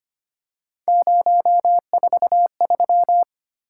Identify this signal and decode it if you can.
Morse code, heard as a keyed tone, about 25 wpm, 700 Hz. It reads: 043
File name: Three Letter-Number Combinations - 25wpm